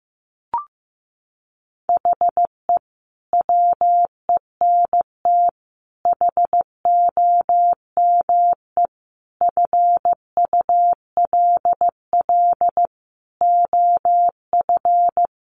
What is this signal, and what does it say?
Morse code, heard as a keyed tone, about 15 wpm, 700 Hz. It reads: HE WENT HOME FULL OF
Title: Sentences from Top 1300 Words 15wpm